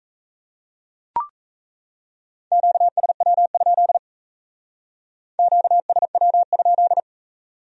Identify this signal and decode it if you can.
Morse code, heard as a keyed tone, about 35 wpm, 700 Hz. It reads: QSW? QSW?
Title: Speed-Racing Q-Codes - 28wpm